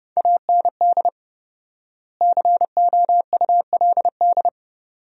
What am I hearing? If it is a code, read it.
Morse code, heard as a keyed tone, about 30 wpm, 700 Hz. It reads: AND COULD
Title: 4x Word-Spacing - Sentences from Top 100 Words - 30wpm